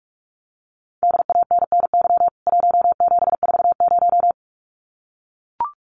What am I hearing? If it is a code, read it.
Morse code, heard as a keyed tone, about 44 wpm, 700 Hz. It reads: DANNY 1740
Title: CWT Exchange 45wpm